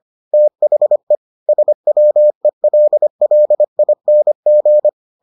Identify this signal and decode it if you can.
Morse code, heard as a keyed tone, about 25 wpm, 600 Hz. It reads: THE SWELLING